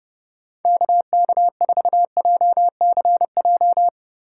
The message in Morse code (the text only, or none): KK4JCJ